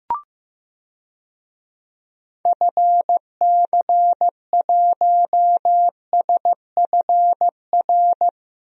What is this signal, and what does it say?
Morse code, heard as a keyed tone, about 15 wpm, 700 Hz. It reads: FC1SFR
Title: DXCC-100 Call Signs - 6 Characters - Repeated 3x - 15wpm